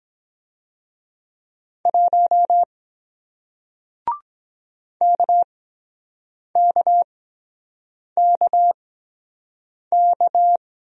Morse code, heard as a keyed tone, about 23 wpm, 700 Hz. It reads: 1 K K K K